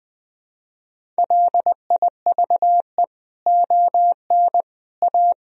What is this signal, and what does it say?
Morse code, heard as a keyed tone, about 19 wpm, 700 Hz. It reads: LIVE ON A
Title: Sentences from Top 1000 Words - Review Part 1 - 20wpm